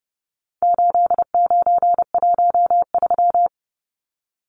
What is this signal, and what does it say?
Morse code, heard as a keyed tone, about 29 wpm, 700 Hz. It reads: 8913
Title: Numbers - 4-Digit 30wpm